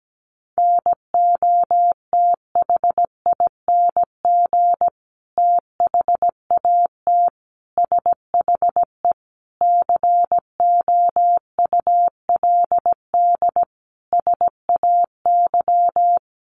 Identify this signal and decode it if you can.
Morse code, heard as a keyed tone, about 17 wpm, 700 Hz. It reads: NOTHING THAT SHE COULD SAY
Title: Sentences from Top 1100 Words 17wpm